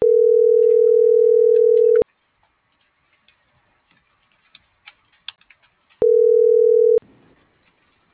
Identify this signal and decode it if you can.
Ringback tone (North America), 440 + 480 Hz.